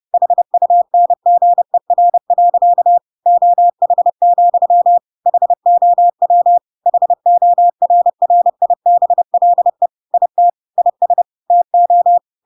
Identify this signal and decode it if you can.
Morse code, heard as a keyed tone, about 30 wpm, 700 Hz. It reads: HUNGER. OH, HOW HORRIBLE IT IS TO
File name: Pinocchio / Pinocchio ch5 30 30 ik1hge